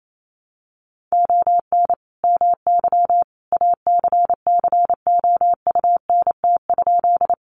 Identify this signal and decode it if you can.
Morse code, heard as a keyed tone, about 28 wpm, 700 Hz. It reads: ON MY ACCOUNT?
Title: Sentences from Top 900 Words 28wpm